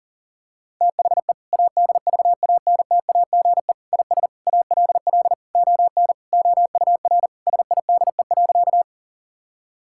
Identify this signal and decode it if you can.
Morse code, heard as a keyed tone, about 37 wpm, 700 Hz. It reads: THE ADVANTAGE IS ALL ON OUR SIDE.
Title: Sentences from Top 1000 Words - Review Part 6 - 40wpm